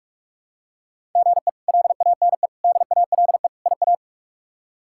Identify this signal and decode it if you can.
Morse code, heard as a keyed tone, about 46 wpm, 700 Hz. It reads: GE LANE DALE IA
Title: K1USN SST QSO Practice - 45wpm